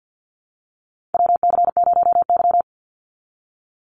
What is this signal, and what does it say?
Morse code, heard as a keyed tone, about 50 wpm, 700 Hz. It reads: WC0Y
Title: NA SOTA Chaser Callsigns 50wpm